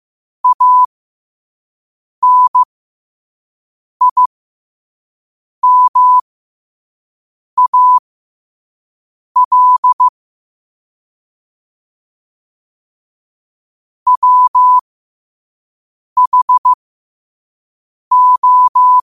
Morse code, heard as a keyed tone, about 15 wpm, spaced as for 5 wpm, 1000 Hz. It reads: ANIMAL WHO
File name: 5wpm